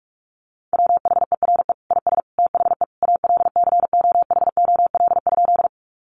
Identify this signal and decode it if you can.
Morse code, heard as a keyed tone, about 45 wpm, 700 Hz. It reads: WHERE IS THE ALCOHOL?
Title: Sentences from Top 1400 Words 45wpm